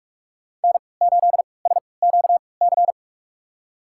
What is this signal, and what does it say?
Morse code, heard as a keyed tone, about 46 wpm, spaced as for 25 wpm, 700 Hz. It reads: N8SQC
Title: USA Call Signs - 5 Characters - 3x Character Spacing - 45wpm